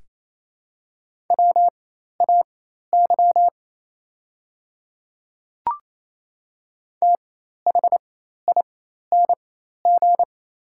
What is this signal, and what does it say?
Morse code, heard as a keyed tone, about 28 wpm, spaced as for 13 wpm, 700 Hz. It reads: WAY THING
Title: 4x Character Spacing - Top 100 Words - 28wpm